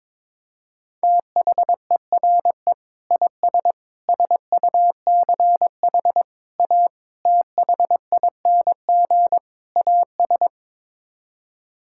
Morse code, heard as a keyed tone, about 22 wpm, 700 Hz. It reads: THERE IS SUCH A THING AS
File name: practice / 22wpm / Sets of 6 Words - 22wpm